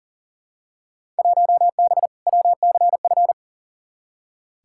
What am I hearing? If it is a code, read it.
Morse code, heard as a keyed tone, about 40 wpm, 700 Hz. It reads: 1B WCF
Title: Field Day Exchanges Encore - 40wpm